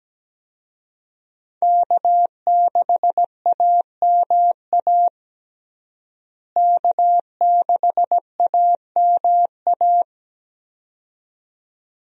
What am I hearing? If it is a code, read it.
Morse code, heard as a keyed tone, about 17 wpm, 700 Hz. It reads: K6AMA K6AMA
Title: USA Call Signs - 5 Characters - Repeated 2x - 17wpm